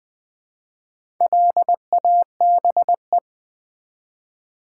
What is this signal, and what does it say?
Morse code, heard as a keyed tone, about 20 wpm, 700 Hz. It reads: LABE
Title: Build A Word - 1 Letter At A Time - 3 to 5-Letter Words - 20wpm